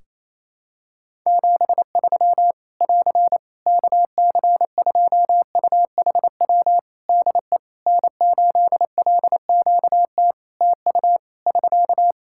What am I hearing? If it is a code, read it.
Morse code, heard as a keyed tone, about 28 wpm, 700 Hz. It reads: 73 <AR> KC2UHW DE N8LQT TU <SK>
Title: Practice QSO - Exchange 4 - Ending - 28wpm